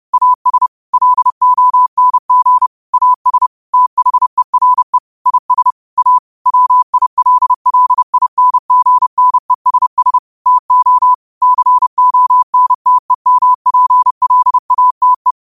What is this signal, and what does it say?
Morse code, heard as a keyed tone, about 30 wpm, 1000 Hz. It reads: AS LONG AS THERE IS A WILLINGNESS TO CONTEMPLATE
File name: qotdfiles / 30wpm